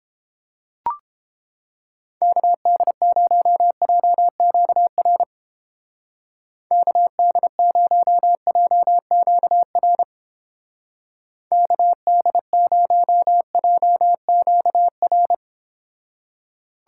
Morse code, heard as a keyed tone, about 29 wpm, 700 Hz. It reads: KD0JQR KD0JQR KD0JQR